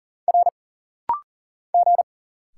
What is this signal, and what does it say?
Morse code, heard as a keyed tone, about 40 wpm, 700 Hz. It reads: R G
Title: Single Letter-Number - Rapid-Fire 40wpm